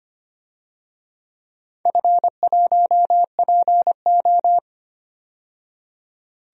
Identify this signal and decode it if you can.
Morse code, heard as a keyed tone, about 25 wpm, 700 Hz. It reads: F1PO
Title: DXCC-100 Call Signs - 4 Characters - 25wpm